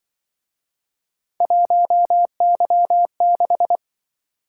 Morse code, heard as a keyed tone, about 24 wpm, 700 Hz. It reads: 1Y6
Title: Speed-Racing - Three Letter-Number Combinations - 20wpm